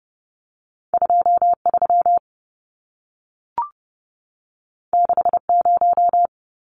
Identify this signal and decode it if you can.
Morse code, heard as a keyed tone, about 30 wpm, 700 Hz. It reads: 23 60
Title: Numbers - 2-Digit 30wpm